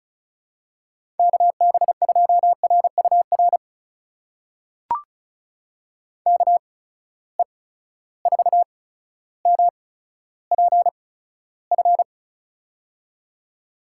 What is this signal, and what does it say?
Morse code, heard as a keyed tone, about 36 wpm, spaced as for 15 wpm, 700 Hz. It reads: KB2RUR KE4MPF
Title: USA Call Signs - 6 Characters - 8x Character Spacing - 35wpm